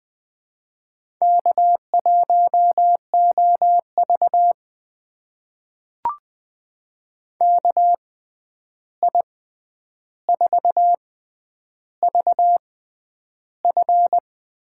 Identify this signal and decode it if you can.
Morse code, heard as a keyed tone, about 20 wpm, spaced as for 10 wpm, 700 Hz. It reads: K1OV KI4VF
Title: USA Call Signs - 6x Character Spacing - 20wpm